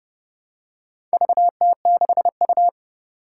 Morse code, heard as a keyed tone, about 30 wpm, 690 Hz. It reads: VT6U
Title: Four Challenging Letter-Number Combinations 30wpm